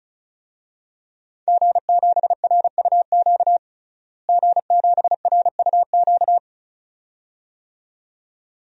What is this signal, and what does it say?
Morse code, heard as a keyed tone, about 34 wpm, 700 Hz. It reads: G7RUQ G7RUQ
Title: DXCC-100 Call Signs - 5 Characters - Repeated 2x - 35wpm